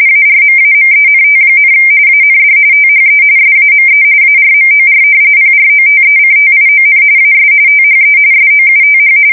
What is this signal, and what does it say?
RTTY, about 45.45 baud. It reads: ARMAMENT, AND THE SAME ARMOR PROTECTION AS THE SOUTH
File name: fsk170 / battleshipnj